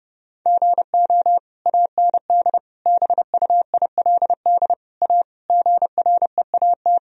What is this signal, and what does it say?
Morse code, heard as a keyed tone, about 30 wpm, 700 Hz. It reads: GO AND BUILD A GREAT
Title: Sentences from Top 1000 Words - Review Part 1 - 30wpm